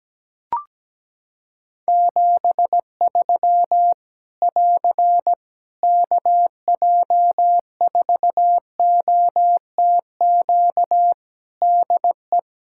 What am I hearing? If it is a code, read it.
Morse code, heard as a keyed tone, about 17 wpm, 700 Hz. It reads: 73 <AR> KJ4OTQ DE
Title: K1USN SST QSO Practice 17wpm